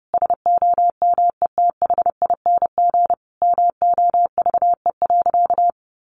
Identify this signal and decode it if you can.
Morse code, heard as a keyed tone, about 28 wpm, 700 Hz. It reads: SOMETHING MOVE.
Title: Sentences from Top 1200 Words 30wpm